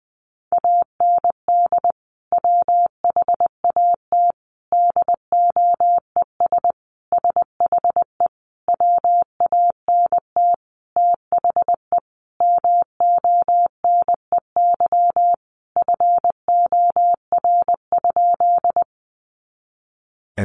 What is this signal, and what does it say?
Morse code, heard as a keyed tone, about 20 wpm, 700 Hz. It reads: AND WHAT DOES SHE WANT THE MONEY FOR?
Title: Sentences from Top 400 Words 20wpm